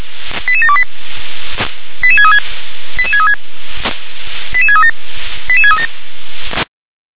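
Five-tone selective call (ZVEI), heard as quick runs of five tones, tones of about 70 ms. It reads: 88728 88636 88636 88637 88637
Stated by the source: ZVEI five-tone call received by SDR.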